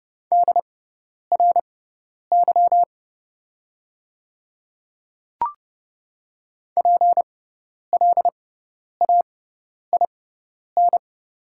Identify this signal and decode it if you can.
Morse code, heard as a keyed tone, about 30 wpm, spaced as for 10 wpm, 700 Hz. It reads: DRY PLAIN
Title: 6x Character Spacing - Top 500 Words - 30wpm